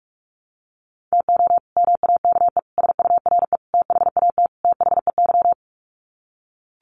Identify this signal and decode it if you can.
Morse code, heard as a keyed tone, about 44 wpm, 700 Hz. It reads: TO MAKE SURE THAT THEY
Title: Sets of 5 Words Encore2 45wpm